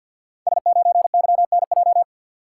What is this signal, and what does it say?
Morse code, heard as a keyed tone, about 51 wpm, 700 Hz. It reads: S9YNJ
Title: Code Groups - 5 Letter:Numbers - 50wpm